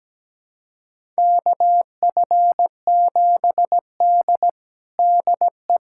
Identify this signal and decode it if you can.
Morse code, heard as a keyed tone, about 17 wpm, 700 Hz. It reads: KF7D DE
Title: USA Callsign DE Callsign - 4 Characters - 17wpm